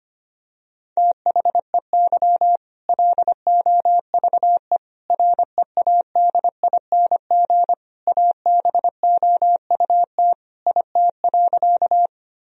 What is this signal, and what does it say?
Morse code, heard as a keyed tone, about 25 wpm, 700 Hz. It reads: THEY LOVE READING ABOUT IT.
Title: Sentences from Top 1000 Words - Review Part 3 - 25wpm